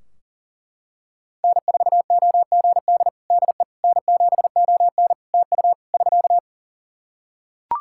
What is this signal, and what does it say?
Morse code, heard as a keyed tone, about 40 wpm, 700 Hz. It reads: N4OGD DE N7ON TU <SK>
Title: Practice QSO - Exchange 4 - Ending - 40wpm